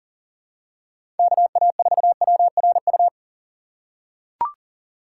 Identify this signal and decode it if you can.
Morse code, heard as a keyed tone, about 40 wpm, 700 Hz. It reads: KA4WRU
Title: USA Call Signs - 6 Characters - 3x Character Spacing - 40wpm